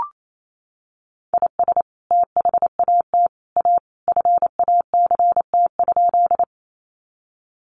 Morse code, heard as a keyed tone, about 28 wpm, 700 Hz. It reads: IS THAT A FACT?
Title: Sentences from Top 500 Words 28wpm